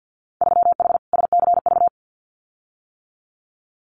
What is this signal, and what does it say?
Morse code, heard as a keyed tone, about 50 wpm, 700 Hz. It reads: 3H SCV
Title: Winter Field Day Exchanges Encore 50wpm